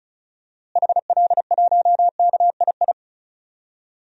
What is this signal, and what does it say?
Morse code, heard as a keyed tone, about 36 wpm, 700 Hz. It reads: HL1KII
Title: DXCC-100 Call Signs - 6 Characters - Repeated 2x - 35wpm